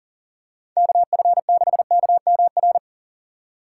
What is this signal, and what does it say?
Morse code, heard as a keyed tone, about 38 wpm, 700 Hz. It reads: KF6KMR
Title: USA Call Signs - 2x Character Spacing - 40wpm